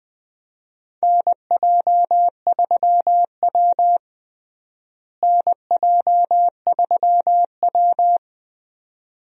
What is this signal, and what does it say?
Morse code, heard as a keyed tone, about 20 wpm, 700 Hz. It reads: NJ3W NJ3W
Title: USA Call Signs - 4 Characters - Repeated 2x - 20wpm